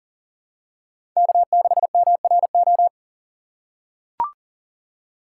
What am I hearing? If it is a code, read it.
Morse code, heard as a keyed tone, about 40 wpm, 700 Hz. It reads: K6MRO
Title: USA Call Signs - 5 Characters - 2x Character Spacing - 40wpm